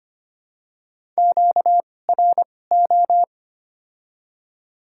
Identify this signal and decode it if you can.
Morse code, heard as a keyed tone, about 25 wpm, spaced as for 18 wpm, 700 Hz. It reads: QRO
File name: Top 90 QSO Elements - 2x Character Spacing - 25wpm